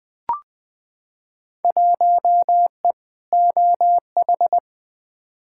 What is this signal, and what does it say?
Morse code, heard as a keyed tone, about 20 wpm, 700 Hz. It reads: 1E OH
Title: Field Day Exchanges - 20wpm